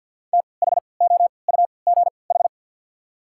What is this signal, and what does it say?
Morse code, heard as a keyed tone, about 53 wpm, spaced as for 29 wpm, 690 Hz. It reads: THOUGH
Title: 3x Character Spacing - Top 500 Words - 50wpm